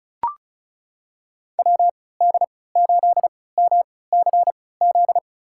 Morse code, heard as a keyed tone, about 36 wpm, spaced as for 20 wpm, 700 Hz. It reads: WD8MCZ
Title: USA Call Signs - 6 Characters - 3x Character Spacing - 35wpm